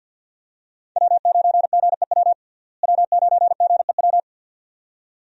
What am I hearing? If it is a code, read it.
Morse code, heard as a keyed tone, about 53 wpm, 700 Hz. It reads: W9GEW W9GEW
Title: USA Call Signs - 5 Characters - Repeated 2x - 50wpm